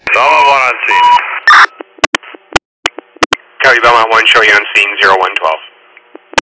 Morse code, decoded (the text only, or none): none